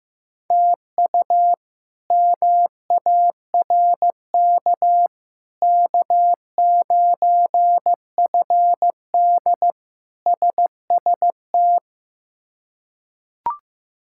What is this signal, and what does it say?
Morse code, heard as a keyed tone, about 15 wpm, 700 Hz. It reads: TU MARK K9FD SST
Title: K1USN SST QSO Practice - 15wpm